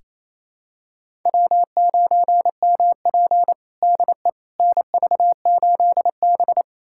Morse code, heard as a keyed tone, about 28 wpm, 700 Hz. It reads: W9MP DE NV8B
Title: USA Callsign DE Callsign - 4 Characters - 28wpm